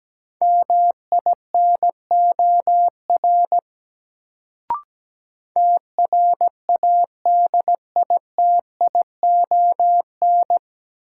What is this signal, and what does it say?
Morse code, heard as a keyed tone, about 17 wpm, 700 Hz. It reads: MINOR TRADITION
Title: Sentences from Top 1500 Words - New Word Drill - 17wpm